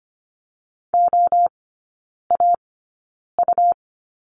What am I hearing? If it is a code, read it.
Morse code, heard as a keyed tone, about 25 wpm, spaced as for 9 wpm, 700 Hz. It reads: OAU
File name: Lesson 17e - Letter u - Three Characters 25wpm